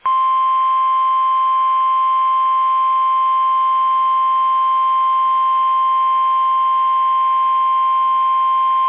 Emergency Alert System (SAME) tones (the 1050 Hz attention tone). It is heard from 1.0 to 8.3 s.